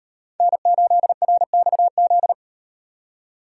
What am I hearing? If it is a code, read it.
Morse code, heard as a keyed tone, about 38 wpm, 700 Hz. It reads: N8RXZ